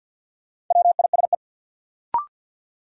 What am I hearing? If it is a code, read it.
Morse code, heard as a keyed tone, about 47 wpm, 700 Hz. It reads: WISE
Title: Sentences from Top 1600 Words - New Word Drill - 50wpm